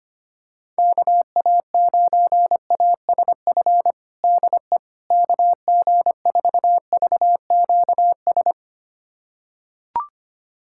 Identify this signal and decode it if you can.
Morse code, heard as a keyed tone, about 25 wpm, 700 Hz. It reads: KA9ASF DE KG4VQS
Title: USA Callsign DE Callsign 25wpm